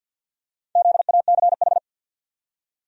Morse code, heard as a keyed tone, about 52 wpm, 700 Hz. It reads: ZACH
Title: USA Common Names - Max 5 Letters - 50wpm